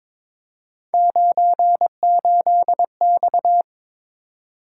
Morse code, heard as a keyed tone, about 22 wpm, 700 Hz. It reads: 98X
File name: practice / 22wpm / Three Letter-Number Combinations - 22wpm